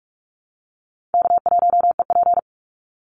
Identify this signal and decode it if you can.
Morse code, heard as a keyed tone, about 44 wpm, 700 Hz. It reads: K1EP
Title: Call Signs - 45wpm